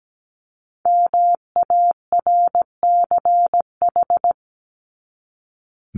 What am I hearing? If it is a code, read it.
Morse code, heard as a keyed tone, about 17 wpm, 700 Hz. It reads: MARCH
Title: Top 1501-1750 Words 17wpm